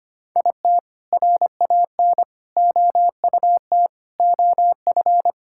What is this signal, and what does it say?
Morse code, heard as a keyed tone, about 25 wpm, 700 Hz. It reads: IT RAN OUT OF
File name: practice / 25wpm / All-Age Appropriate Jokes - 25wpm